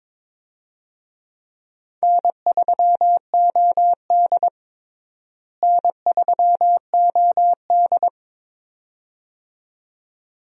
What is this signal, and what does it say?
Morse code, heard as a keyed tone, about 22 wpm, 700 Hz. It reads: N3OD N3OD
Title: USA Call Signs - 4 Characters - Repeated 2x - 22wpm